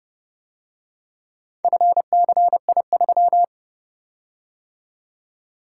Morse code, heard as a keyed tone, about 29 wpm, 700 Hz. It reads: FCI3